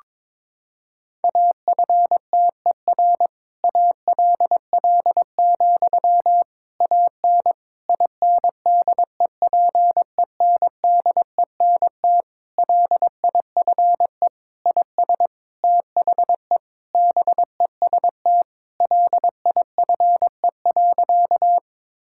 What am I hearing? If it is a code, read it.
Morse code, heard as a keyed tone, about 22 wpm, 700 Hz. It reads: AFTER ALL, AN INDEPENDENT LIFE IS THE BEST LIFE.
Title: Sentences from Top 1000 Words - Review Part 6 - 22wpm